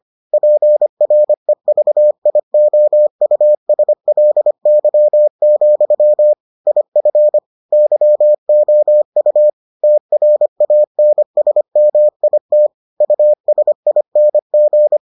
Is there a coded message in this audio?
Yes: Morse code from 0.3 to 15.0 s, heard as a keyed tone, about 25 wpm, 600 Hz. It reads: PREVIOUSLY, IF YOU TRANSMIT USING